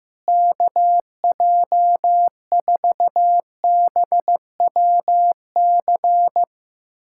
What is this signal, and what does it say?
Morse code, heard as a keyed tone, about 15 wpm, 700 Hz. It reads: KJ4BWC
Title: USA Callsign DE Callsign 15wpm